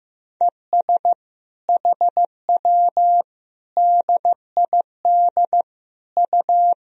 Morse code, heard as a keyed tone, about 15 wpm, 700 Hz. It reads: ES HW DID U
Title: Sentences from Top 100 Words and Abbreviations 15wpm